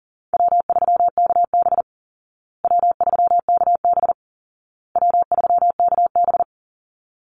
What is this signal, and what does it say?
Morse code, heard as a keyed tone, about 40 wpm, 700 Hz. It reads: W3KB W3KB W3KB
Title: Easy Call Signs Repeated 3x - 40wpm